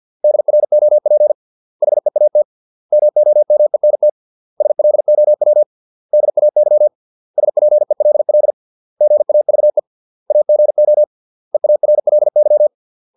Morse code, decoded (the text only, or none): DROP HEAT MOMENT SLOW DAY SPELL GAVE AGO EARLY